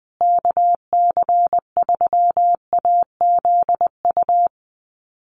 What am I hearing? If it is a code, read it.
Morse code, heard as a keyed tone, about 19 wpm, 700 Hz. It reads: KC3AZU
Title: USA Call Signs Repeated 2x 20wpm